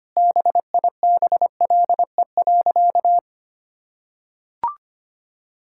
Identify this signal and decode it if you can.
Morse code, heard as a keyed tone, about 25 wpm, 700 Hz. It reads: BIBLE.
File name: practice / 25wpm / Sentences with 9-Word Phrases - 25wpm